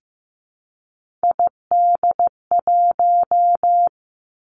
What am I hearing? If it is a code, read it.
Morse code, heard as a keyed tone, about 15 wpm, 700 Hz. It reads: ID1
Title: Three Challenging Letter-Number Combinations 15wpm